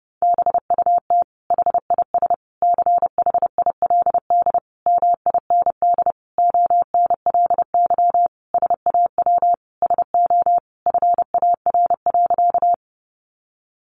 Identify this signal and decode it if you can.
Morse code, heard as a keyed tone, about 30 wpm, 700 Hz. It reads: BUT HIS CHILD MIND ONLY SAW SO FAR.
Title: Sentences from Top 800 Words 30wpm